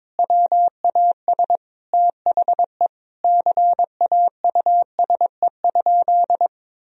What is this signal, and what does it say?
Morse code, heard as a keyed tone, about 21 wpm, 700 Hz. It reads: WAS THE CAUSE?
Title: Sentences from Top 1000 Words - Review Part 1 - 22wpm